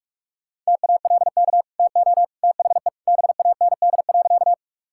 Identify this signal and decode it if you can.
Morse code, heard as a keyed tone, about 46 wpm, 700 Hz. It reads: TALK TO THE BAND.
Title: Sentences from Top 1600 Words 45wpm